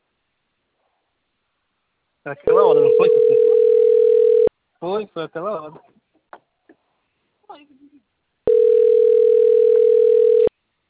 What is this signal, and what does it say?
Ringback tone (North America), 440 + 480 Hz, with speech.
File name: rg-1600-unknown- - - .22355